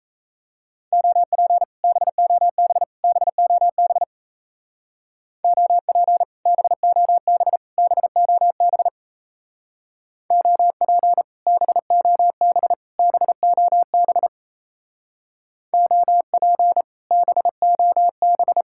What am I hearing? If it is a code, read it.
Morse code, heard as a keyed tone, about 33 wpm, 700 Hz. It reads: OP BOB BOB OP BOB BOB OP BOB BOB OP BOB
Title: Speed-Racing Common QSO Phrases - 28wpm